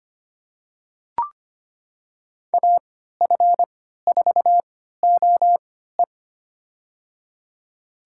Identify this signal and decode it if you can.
Morse code, heard as a keyed tone, about 25 wpm, spaced as for 14 wpm, 700 Hz. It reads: AF4OE
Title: USA Call Signs - 5 Characters - 3x Character Spacing - 25wpm